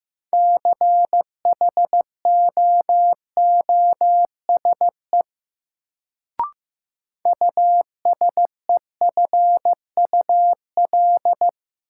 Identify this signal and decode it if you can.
Morse code, heard as a keyed tone, about 16 wpm, 700 Hz. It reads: CHOOSE USEFUL
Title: Sentences from Top 700 Words - New Word Drill - 15wpm